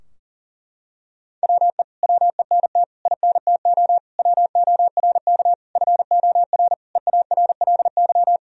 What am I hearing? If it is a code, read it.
Morse code, heard as a keyed tone, about 40 wpm, 700 Hz. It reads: WE WENT INTO WORK FOR EARLY
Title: Sentences from Top 1000 Words - Review Part 4 - 40wpm